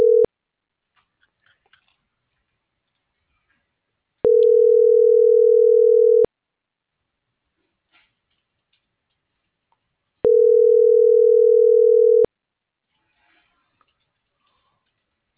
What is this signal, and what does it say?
Ringback tone (North America), 440 + 480 Hz.